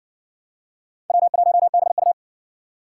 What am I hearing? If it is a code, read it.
Morse code, heard as a keyed tone, about 57 wpm, 700 Hz. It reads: W1BU